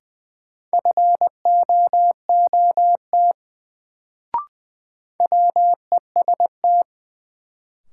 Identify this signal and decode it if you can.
Morse code, heard as a keyed tone, about 20 wpm, 700 Hz. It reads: FOOT WEST
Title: Sets of 5 Words - New Word Drill - 20wpm